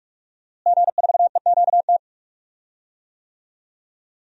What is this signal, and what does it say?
Morse code, heard as a keyed tone, about 43 wpm, 700 Hz. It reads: G4EQT
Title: DXCC-100 Call Signs - 5 Characters - Repeated 2x - 45wpm